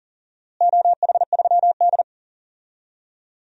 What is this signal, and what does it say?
Morse code, heard as a keyed tone, about 37 wpm, 700 Hz. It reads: OH3D
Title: DXCC-100 Call Signs - 4 Characters - Repeated 2x - 40wpm